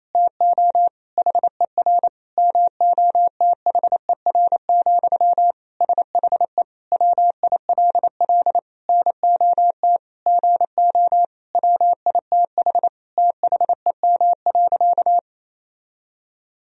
Morse code, heard as a keyed tone, about 28 wpm, 700 Hz. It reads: TO HER MOTHER, SHE WILL NOT GO WITH THEM.
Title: Sentences from Top 1000 Words - Review Part 4 - 28wpm